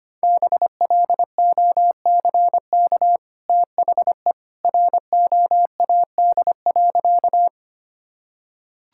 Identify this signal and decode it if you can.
Morse code, heard as a keyed tone, about 25 wpm, 700 Hz. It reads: BLOCK THE ROAD.
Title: Sentences from Top 1600 Words 25wpm